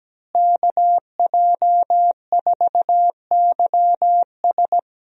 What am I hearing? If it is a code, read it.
Morse code, heard as a keyed tone, about 17 wpm, 700 Hz. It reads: KJ4YS
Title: USA Call Signs - 5 Characters - Repeated 3x - 17wpm